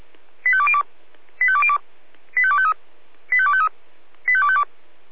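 Five-tone selective call (ZVEI), heard as quick runs of five tones, tones of about 70 ms. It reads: 85202 85202 85253 85253 85252
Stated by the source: Received by SDR as a ZVEI five-tone selective call.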